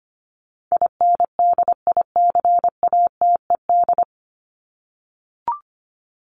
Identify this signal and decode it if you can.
Morse code, heard as a keyed tone, about 25 wpm, 700 Hz. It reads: INDICATED
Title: ED Suffix - Top 100 Words with Base Words 25wpm